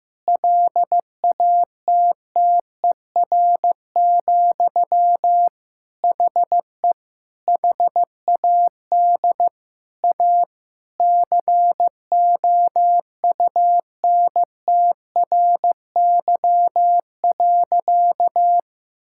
Morse code, heard as a keyed tone, about 16 wpm, 700 Hz. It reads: LATTER, HE HAD A COUNTRY.
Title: Sentences from Top 1600 Words 15wpm